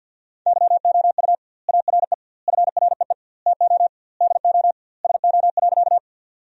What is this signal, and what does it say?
Morse code, heard as a keyed tone, about 52 wpm, 700 Hz. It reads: YOU ARE FREE TO DO SO.
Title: 2x Word-Spacing - Sentences from Top 500 Words - 50wpm